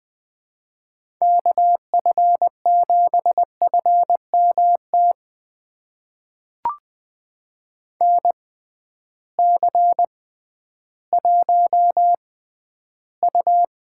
Morse code, heard as a keyed tone, about 20 wpm, spaced as for 12 wpm, 700 Hz. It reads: KF7FMT NC1U
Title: USA Call Signs - 6x Character Spacing - 20wpm